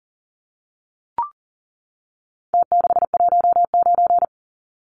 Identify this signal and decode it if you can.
Morse code, heard as a keyed tone, about 37 wpm, 700 Hz. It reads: T619
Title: Numbers - Contest Style - 4-Digit 40wpm